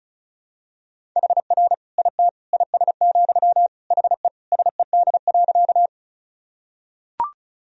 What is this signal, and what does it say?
Morse code, heard as a keyed tone, about 36 wpm, 700 Hz. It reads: HR IT IS, HE SED.
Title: Sentences from Top 200 Words and Abbreviations 35wpm